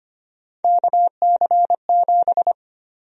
Morse code, heard as a keyed tone, about 25 wpm, 700 Hz. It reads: KC7
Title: USA Call Sign Prefixes - 25wpm